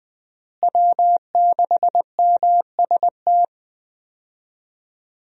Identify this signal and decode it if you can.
Morse code, heard as a keyed tone, about 20 wpm, 700 Hz. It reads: W6MST
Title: USA Call Signs - 5 Characters - Repeated 3x - 20wpm